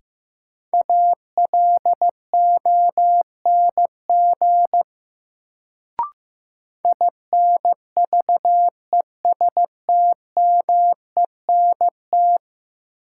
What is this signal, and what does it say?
Morse code, heard as a keyed tone, about 16 wpm, 700 Hz. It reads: ALONG INVESTMENT
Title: Sentences from Top 700 Words - New Word Drill - 15wpm